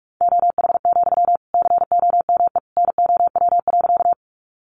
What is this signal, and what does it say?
Morse code, heard as a keyed tone, about 44 wpm, 700 Hz. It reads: OH, COME NOW.
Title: Sentences from Top 500 Words 45wpm